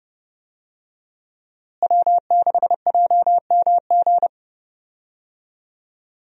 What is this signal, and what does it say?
Morse code, heard as a keyed tone, about 32 wpm, 700 Hz. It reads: W6JMG
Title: USA Call Signs - 5 Characters - Repeated 2x - 30wpm